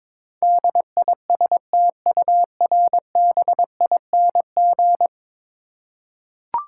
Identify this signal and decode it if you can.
Morse code, heard as a keyed tone, about 22 wpm, 700 Hz. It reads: DISTURBING
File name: practice / 22wpm / 10 Letter Words - First 3 Letters Given - 22wpm